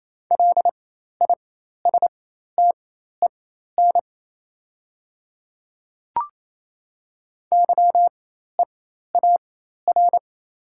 Morse code, heard as a keyed tone, about 28 wpm, spaced as for 13 wpm, 700 Hz. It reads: LISTEN YEAR
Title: 4x Character Spacing - Top 500 Words - 28wpm